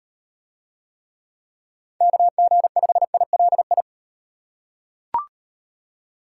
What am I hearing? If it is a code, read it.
Morse code, heard as a keyed tone, about 38 wpm, 700 Hz. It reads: KG5ILI